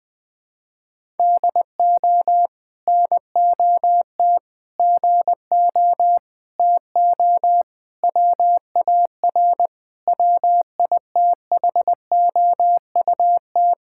Morse code, heard as a keyed tone, about 20 wpm, 700 Hz. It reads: DO NOT GO TO WAR WITHOUT
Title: Sentences from Top 1000 Words - Review Part 5 - 20wpm